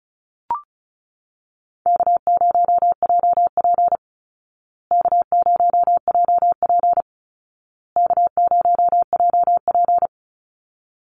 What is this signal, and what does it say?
Morse code, heard as a keyed tone, about 34 wpm, 700 Hz. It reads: K0JP K0JP K0JP
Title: Easy Call Signs Repeated 3x - 35wpm